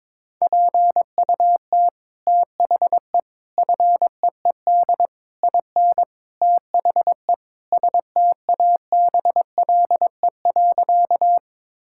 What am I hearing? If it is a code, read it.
Morse code, heard as a keyed tone, about 22 wpm, 700 Hz. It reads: PUT THE FEED IN THE STABLE.
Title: Sentences from Top 1600 Words 22wpm